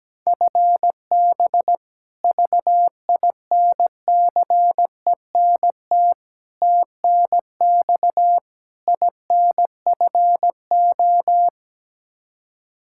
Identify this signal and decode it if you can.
Morse code, heard as a keyed tone, about 17 wpm, 700 Hz. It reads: FB VINCENT TNX INFO
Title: Practice QSO - Exchange 3 - More Info - 17wpm